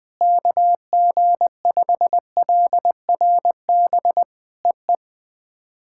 Morse code, heard as a keyed tone, about 19 wpm, 700 Hz. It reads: KG5LRB EE
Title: POTA QSO Practice - 20wpm